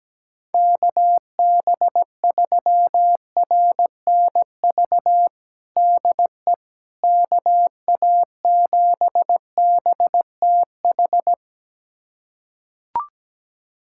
Morse code, heard as a keyed tone, about 17 wpm, 700 Hz. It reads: KB3RNV DE KA7BTH
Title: USA Callsign DE Callsign 17wpm